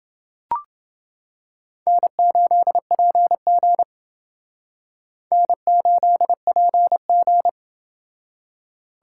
Morse code, heard as a keyed tone, about 28 wpm, 700 Hz. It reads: N8PG N8PG